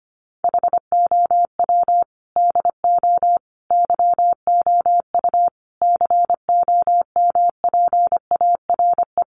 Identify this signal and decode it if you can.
Morse code, heard as a keyed tone, about 25 wpm, 700 Hz. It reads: HOW DO YOU COMPARE
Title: Sentences from Top 1300 Words 25wpm